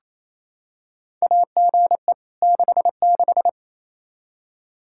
Morse code, heard as a keyed tone, about 28 wpm, 700 Hz. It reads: AGE 66
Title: Practice QSO - Exchange 3 - More Info - 28wpm